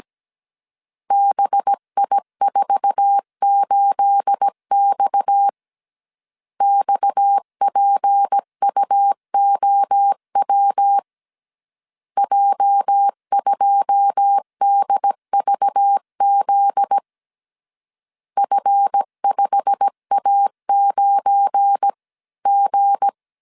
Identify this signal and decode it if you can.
Morse code, heard as a keyed tone, about 17 wpm, 800 Hz. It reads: BI48X XPUOW J2DVZ F5A9G
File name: CW / DET / 1 9 1500C S C 16wpm 2909